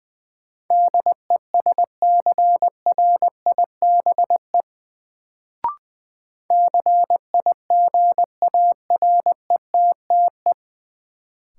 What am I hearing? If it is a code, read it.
Morse code, heard as a keyed tone, about 21 wpm, 700 Hz. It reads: DESCRIBE CIGARETTE
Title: Sentences from Top 1000 Words - New Word Drill - 20wpm